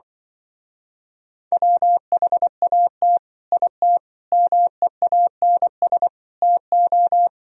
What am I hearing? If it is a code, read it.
Morse code, heard as a keyed tone, about 24 wpm, 700 Hz. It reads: WHAT IT MEANS TO
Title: Speed-Racing - Sets of 5 Words 20wpm